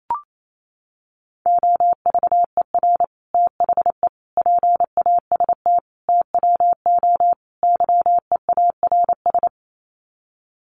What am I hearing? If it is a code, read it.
Morse code, heard as a keyed tone, about 28 wpm, 700 Hz. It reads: OVER THE PAST TWO YEARS
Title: Sets of 5 Words Encore 28wpm